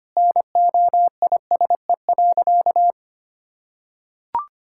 Morse code, heard as a keyed tone, about 25 wpm, 700 Hz. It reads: NOISE.
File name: practice / 25wpm / Sentences with 2-Word Phrases - 25wpm